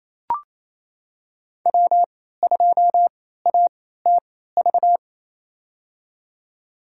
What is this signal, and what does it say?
Morse code, heard as a keyed tone, about 28 wpm, spaced as for 16 wpm, 700 Hz. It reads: W2ATV
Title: USA Call Signs - 3x Character Spacing - 28wpm